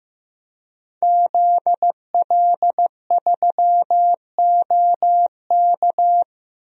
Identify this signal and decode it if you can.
Morse code, heard as a keyed tone, about 15 wpm, 700 Hz. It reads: ZL3OK
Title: DXCC-100 Call Signs - 5 Characters - Repeated 2x - 15wpm